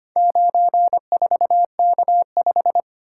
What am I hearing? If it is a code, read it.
Morse code, heard as a keyed tone, about 25 wpm, 700 Hz. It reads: 94K5
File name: practice / 25wpm / Four Letter-Number Combinations - 25wpm